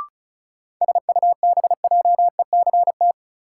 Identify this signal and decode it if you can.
Morse code, heard as a keyed tone, about 34 wpm, 700 Hz. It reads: SUBJECT
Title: Sentences from Top 800 Words - New Word Drill - 35wpm